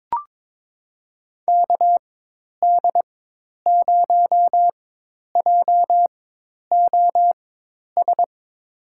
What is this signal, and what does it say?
Morse code, heard as a keyed tone, about 22 wpm, spaced as for 10 wpm, 700 Hz. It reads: KD0JOS
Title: USA Call Signs - 4x Character Spacing - 22wpm